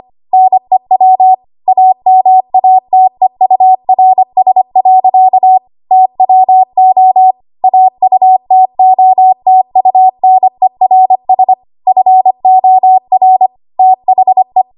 Morse code, heard as a keyed tone, about 25 wpm, 750 Hz. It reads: NEW AMATEURS. TWO AUTOTUNERS FOR THE
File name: Archive / 25 WPM / 25WPM